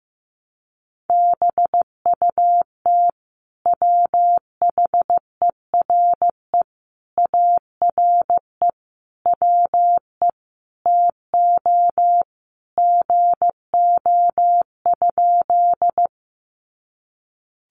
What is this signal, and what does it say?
Morse code, heard as a keyed tone, about 15 wpm, 700 Hz. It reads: BUT WHERE ARE WE TO GO?
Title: Sentences from Top 200 Words 15wpm